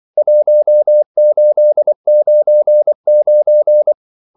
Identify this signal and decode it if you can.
Morse code, heard as a keyed tone, about 24 wpm, 600 Hz. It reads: 1899
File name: CW / Practice / EnglishQuotations0424